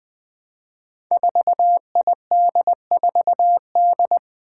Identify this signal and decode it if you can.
Morse code, heard as a keyed tone, about 20 wpm, 700 Hz. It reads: 4ID4D
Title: Five Challenging Letter-Number Combinations 20wpm